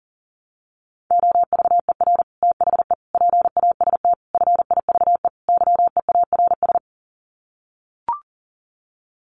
Morse code, heard as a keyed tone, about 40 wpm, 700 Hz. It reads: OVER THE PAST FIVE YEARS
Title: Sets of 5 Words Encore 40wpm